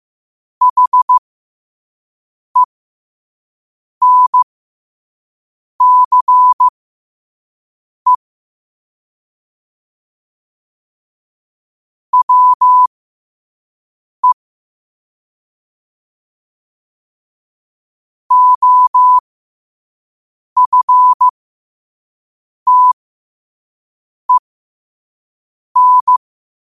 Morse code, heard as a keyed tone, about 15 wpm, spaced as for 5 wpm, 1000 Hz. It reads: HENCE WE OFTEN
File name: qotdfiles / 5wpm